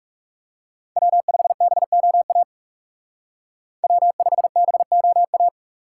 Morse code, heard as a keyed tone, about 35 wpm, 700 Hz. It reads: W5BOA W5BOA
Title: Speed Racing - USA 1x3 Call Signs 30wpm